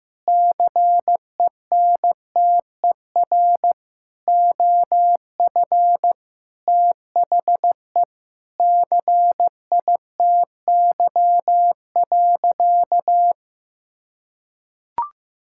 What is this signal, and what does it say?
Morse code, heard as a keyed tone, about 15 wpm, 700 Hz. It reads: CENTER OF THE CITY.
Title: Sentences from Top 1000 Words - Review Part 6 - 15wpm